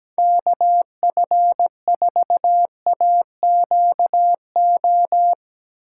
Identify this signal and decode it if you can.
Morse code, heard as a keyed tone, about 18 wpm, 700 Hz. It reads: KF4AQO
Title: Speed Racing - USA 2x3 Call Signs 17wpm